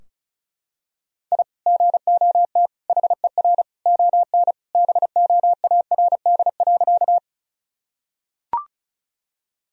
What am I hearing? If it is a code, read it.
Morse code, heard as a keyed tone, about 36 wpm, 700 Hz. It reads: I GOT HER ON BOARD.
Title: Sentences from Top 1000 Words - Review Part 4 - 35wpm